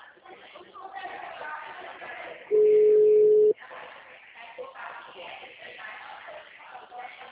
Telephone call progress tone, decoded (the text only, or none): ringback tone